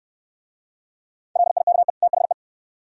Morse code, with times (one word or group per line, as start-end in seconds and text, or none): 1.4-1.9 LIKE
2.0-2.3 THE